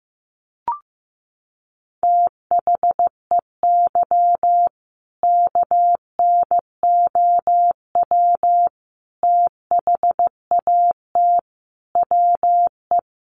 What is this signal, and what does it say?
Morse code, heard as a keyed tone, about 15 wpm, 700 Hz. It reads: THEY KNOW THAT WE
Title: Sentences from Top 1400 Words 15wpm